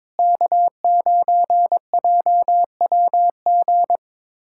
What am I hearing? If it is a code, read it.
Morse code, heard as a keyed tone, about 22 wpm, 700 Hz. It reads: K9JWG
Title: USA Callsign DE Callsign 22wpm